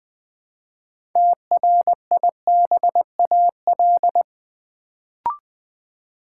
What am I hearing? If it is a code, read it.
Morse code, heard as a keyed tone, about 20 wpm, 700 Hz. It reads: TRIBAL
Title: Build A Word - 3 Letters At A Time - 6-Letter Words - 20wpm